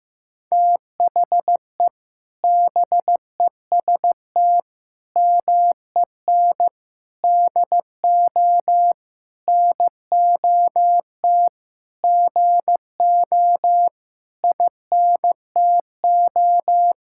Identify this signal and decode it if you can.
Morse code, heard as a keyed tone, about 15 wpm, 700 Hz. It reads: THE BEST MEN DO NOT GO INTO
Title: Sentences from Top 1000 Words - Review Part 6 - 15wpm